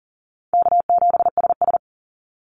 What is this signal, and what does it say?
Morse code, heard as a keyed tone, about 41 wpm, 700 Hz. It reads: K7SS
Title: Call Signs - 40wpm